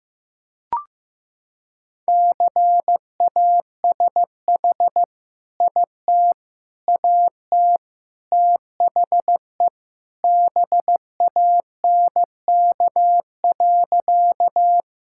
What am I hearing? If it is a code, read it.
Morse code, heard as a keyed tone, about 15 wpm, 700 Hz. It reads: CASH IT AT THE BANK.
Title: Sentences from Top 1000 Words - Review Part 5 - 15wpm